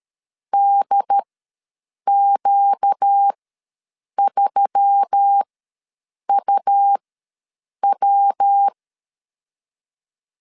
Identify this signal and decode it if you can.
Morse code, heard as a keyed tone, about 13 wpm, spaced as for 7 wpm, 800 Hz. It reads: DQ3UW
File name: LEZIONI / 8 YJ45 / 8 YJ45 ws 12wpm 3146